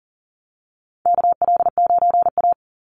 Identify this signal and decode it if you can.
Morse code, heard as a keyed tone, about 40 wpm, 700 Hz. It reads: KL9A
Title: Call Signs - 40wpm